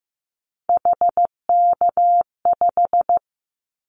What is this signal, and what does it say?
Morse code, heard as a keyed tone, about 15 wpm, 700 Hz. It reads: HK5
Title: Callsign Prefixes for Top 100 DXCC - 15wpm